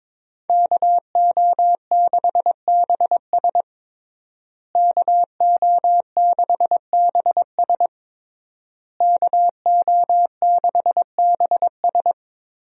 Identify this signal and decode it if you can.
Morse code, heard as a keyed tone, about 22 wpm, 700 Hz. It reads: KO6BS KO6BS KO6BS
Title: USA Call Signs - 5 Characters - Repeated 3x - 22wpm